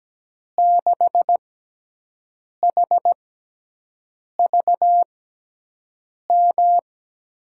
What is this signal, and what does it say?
Morse code, heard as a keyed tone, about 17 wpm, spaced as for 6 wpm, 700 Hz. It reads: 6HVM
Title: USA Call Signs - 6 Characters - 6x Character Spacing - 17wpm